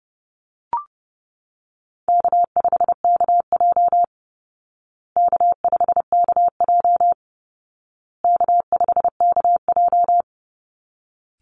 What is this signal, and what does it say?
Morse code, heard as a keyed tone, about 30 wpm, 700 Hz. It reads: K5KJ K5KJ K5KJ
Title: Call Signs Repeated 2x - 30wpm